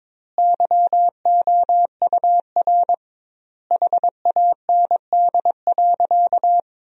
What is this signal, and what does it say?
Morse code, heard as a keyed tone, about 22 wpm, 700 Hz. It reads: YOUR HAND.
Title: 2x Word-Spacing - Sentences from Top 400 Words - 22wpm